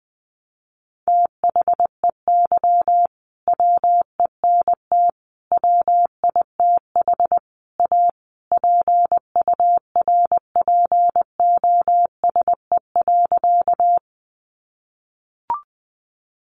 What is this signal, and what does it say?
Morse code, heard as a keyed tone, about 20 wpm, 700 Hz. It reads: THEY WENT WITH A PURPOSE.
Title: Sentences from Top 800 Words 20wpm